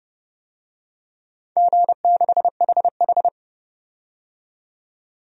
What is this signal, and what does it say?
Morse code, heard as a keyed tone, about 29 wpm, 700 Hz. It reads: G6HH
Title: DXCC-100 Call Signs - 4 Characters - 30wpm